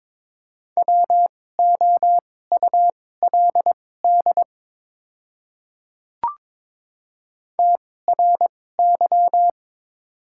Morse code, heard as a keyed tone, about 22 wpm, spaced as for 16 wpm, 700 Hz. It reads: WOULD TRY
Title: 2x Character Spacing - Top 500 Words - 22wpm